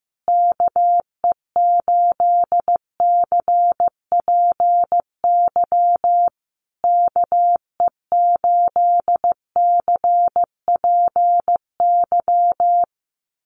Morse code, heard as a keyed tone, about 15 wpm, 700 Hz. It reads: KE8CPY KE8CPY
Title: USA Call Signs Repeated 3x 15wpm